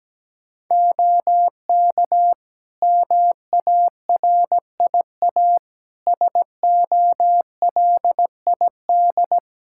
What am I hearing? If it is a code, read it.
Morse code, heard as a keyed tone, about 17 wpm, 700 Hz. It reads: OK MARIA SOLID
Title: Practice QSO - Exchange 3 - More Info - 17wpm